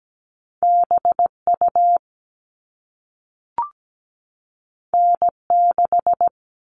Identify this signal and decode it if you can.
Morse code, heard as a keyed tone, about 17 wpm, 700 Hz. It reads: BU N6
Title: All Cut Numbers - 2-Digit 17wpm